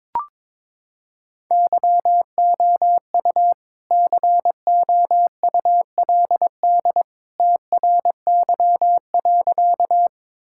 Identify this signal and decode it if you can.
Morse code, heard as a keyed tone, about 22 wpm, 700 Hz. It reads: YOU COULD TRY.
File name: practice / 22wpm / Sentences with 2-Word Phrases - 22wpm